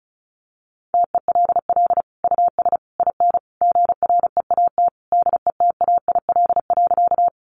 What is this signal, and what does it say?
Morse code, heard as a keyed tone, about 34 wpm, 700 Hz. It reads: TELL US IN GREAT DETAIL.
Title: Sentences from Top 1500 Words 35wpm